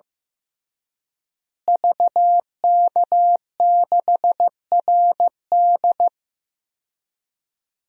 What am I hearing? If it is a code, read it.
Morse code, heard as a keyed tone, about 15 wpm, 700 Hz. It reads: VK6RD
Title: DXCC-100 Call Signs - 5 Characters - Repeated 3x - 15wpm